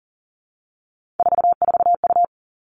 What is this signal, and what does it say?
Morse code, heard as a keyed tone, about 38 wpm, 690 Hz. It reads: 44U
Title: Three Challenging Letter-Number Combinations 40wpm